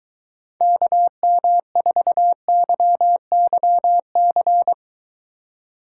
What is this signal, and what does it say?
Morse code, heard as a keyed tone, about 23 wpm, 700 Hz. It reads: KM4YYC